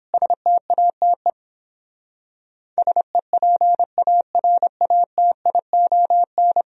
Morse code, heard as a keyed tone, about 31 wpm, 700 Hz. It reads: STATE SEPARATION